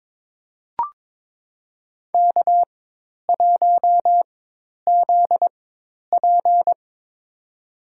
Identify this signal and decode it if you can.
Morse code, heard as a keyed tone, about 22 wpm, spaced as for 10 wpm, 700 Hz. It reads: K1ZP
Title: USA Call Signs - 4 Characters - 4x Character Spacing - 22wpm